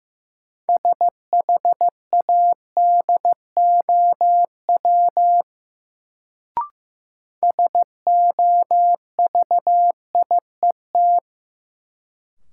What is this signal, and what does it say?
Morse code, heard as a keyed tone, about 15 wpm, 700 Hz. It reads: SHADOW SOVIET
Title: Sets of 5 Words Encore - New Word Drill - 15wpm